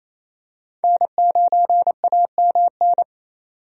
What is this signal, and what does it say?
Morse code, heard as a keyed tone, about 28 wpm, 700 Hz. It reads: N9AMN
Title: USA Call Signs - 5 Characters - 28wpm